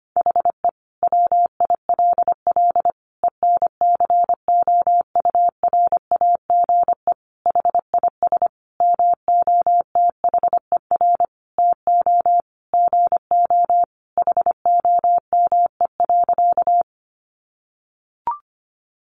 Morse code, heard as a keyed tone, about 25 wpm, 700 Hz. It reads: HE WILL ENCOURAGE HIS MOTHER TO GO HOME.
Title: Sentences from Top 1400 Words 25wpm